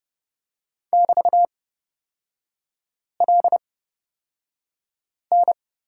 Morse code, heard as a keyed tone, about 29 wpm, spaced as for 5 wpm, 700 Hz. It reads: <BT>LN
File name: Lesson 36e - BT - Three Characters 30wpm